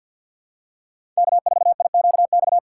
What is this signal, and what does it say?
Morse code, heard as a keyed tone, about 52 wpm, 700 Hz. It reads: K4IQX
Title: USA Call Signs - 5 Characters - 6x Character Spacing - 50wpm